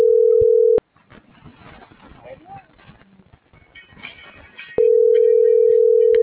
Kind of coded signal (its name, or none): telephone call progress tone